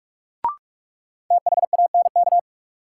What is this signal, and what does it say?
Morse code, heard as a keyed tone, about 41 wpm, 700 Hz. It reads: THANK
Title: Sets of 2 Words - New Word Drill - 45wpm